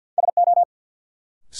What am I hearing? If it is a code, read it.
Morse code, heard as a keyed tone, about 53 wpm, 700 Hz. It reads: SO
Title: Sentences from Top 100 Words - New Word Drill - 50wpm